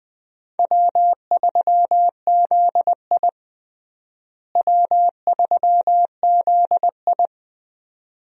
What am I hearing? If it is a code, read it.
Morse code, heard as a keyed tone, about 20 wpm, 700 Hz. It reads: W3ZI W3ZI
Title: USA Call Signs - 4 Characters - Repeated 2x - 20wpm